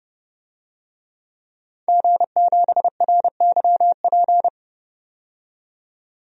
Morse code, heard as a keyed tone, about 32 wpm, 700 Hz. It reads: G7RYP
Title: DXCC-100 Call Signs - 5 Characters - 30wpm